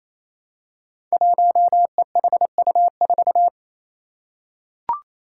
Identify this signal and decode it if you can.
Morse code, heard as a keyed tone, about 28 wpm, 700 Hz. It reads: 1EHU4
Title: Five Challenging Letter-Number Combinations 28wpm